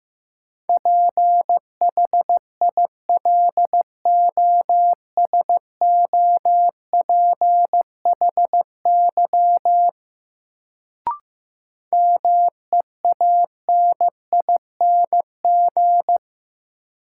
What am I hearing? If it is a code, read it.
Morse code, heard as a keyed tone, about 15 wpm, 700 Hz. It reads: PHILOSOPHY MEANING
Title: Sentences from Top 900 Words - New Word Drill - 15wpm